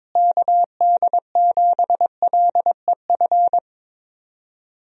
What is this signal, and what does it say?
Morse code, heard as a keyed tone, about 22 wpm, 700 Hz. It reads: KD7LEF
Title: USA Call Signs - 6 Characters - Repeated 2x - 22wpm